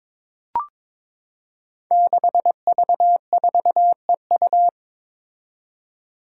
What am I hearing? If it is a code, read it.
Morse code, heard as a keyed tone, about 22 wpm, 700 Hz. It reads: 6V4EU
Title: Five Challenging Letter-Number Combinations 22wpm